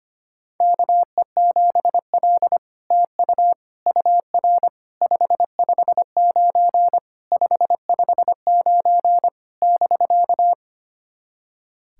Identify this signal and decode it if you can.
Morse code, heard as a keyed tone, about 25 wpm, 700 Hz. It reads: KE7L TU UR 559 559 <BK>
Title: SOTA QSO Practice 25wpm